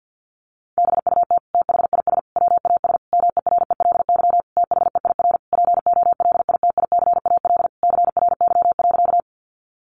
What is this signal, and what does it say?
Morse code, heard as a keyed tone, about 50 wpm, 700 Hz. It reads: BUT THIS WAS MERELY THEIR POLITICAL CRY.
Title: Sentences from Top 1400 Words 50wpm